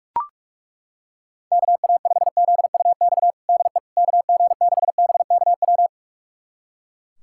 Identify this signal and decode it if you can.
Morse code, heard as a keyed tone, about 41 wpm, 700 Hz. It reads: KA5ZUX DE KG6BKW
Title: USA Callsign DE Callsign 45wpm